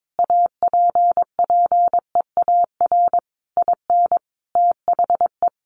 Morse code, heard as a keyed tone, about 22 wpm, 700 Hz. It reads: APPEAR IN THE
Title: Sentences from Top 1200 Words 22wpm